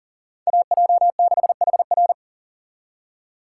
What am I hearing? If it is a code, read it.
Morse code, heard as a keyed tone, about 51 wpm, 700 Hz. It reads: AJ6HR